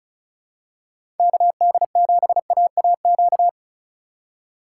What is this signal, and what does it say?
Morse code, heard as a keyed tone, about 34 wpm, 700 Hz. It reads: KD7AAQ
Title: USA Call Signs - 6x Character Spacing - 35wpm